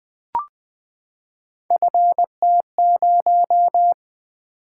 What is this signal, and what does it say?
Morse code, heard as a keyed tone, about 20 wpm, 700 Hz. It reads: FT0
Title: Three Letter-Number Combinations - 20wpm